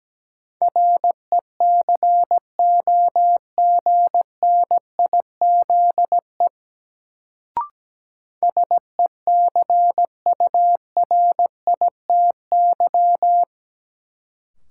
Morse code, heard as a keyed tone, about 17 wpm, 700 Hz. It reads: RECOGNIZE SECURITY
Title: Sentences from Top 1400 Words - New Word Drill Alt - 17wpm